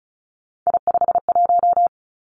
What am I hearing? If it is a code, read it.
Morse code, heard as a keyed tone, about 36 wpm, 700 Hz. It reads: I51
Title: Three Challenging Letter-Number Combinations 35wpm